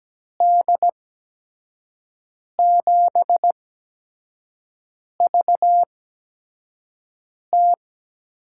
Morse code, heard as a keyed tone, about 17 wpm, spaced as for 5 wpm, 700 Hz. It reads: D7VT
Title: USA Call Signs - 6 Characters - 8x Character Spacing - 17wpm